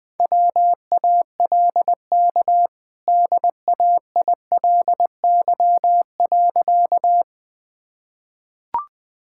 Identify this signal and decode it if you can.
Morse code, heard as a keyed tone, about 21 wpm, 700 Hz. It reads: WALK DAILY.
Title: Easy Sentences with Hints - 20wpm